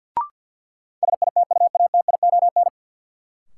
Morse code, heard as a keyed tone, about 53 wpm, 700 Hz. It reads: SITUATION
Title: Sentences from Top 1200 Words - New Word Drill - 50wpm